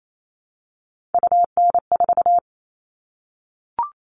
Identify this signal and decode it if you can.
Morse code, heard as a keyed tone, about 28 wpm, 700 Hz. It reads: UN4
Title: All Cut Numbers - 3-Digit 28wpm